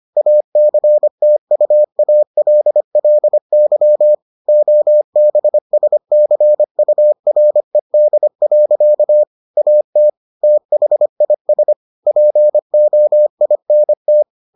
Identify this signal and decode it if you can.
Morse code, heard as a keyed tone, about 25 wpm, 600 Hz. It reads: ACTUALLY OBSCURED. AT THIS POINT